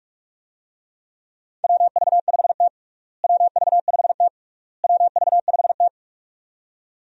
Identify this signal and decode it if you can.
Morse code, heard as a keyed tone, about 46 wpm, 700 Hz. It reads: WV5T WV5T WV5T
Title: USA Call Signs - 4 Characters - Repeated 3x - 45wpm